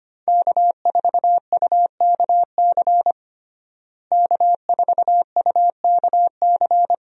Morse code, heard as a keyed tone, about 25 wpm, 700 Hz. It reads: K4UKC K4UKC
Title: USA Call Signs - 5 Characters - Repeated 3x - 25wpm